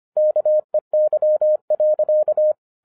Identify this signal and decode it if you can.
Morse code, heard as a keyed tone, about 25 wpm, 610 Hz. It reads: KEY.
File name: FIRST FISTS CONTACT-25wpm